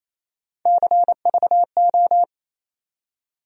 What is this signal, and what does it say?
Morse code, heard as a keyed tone, about 28 wpm, 700 Hz. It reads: CVO
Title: USA Call Sign Postfixes - 3 Letters 28wpm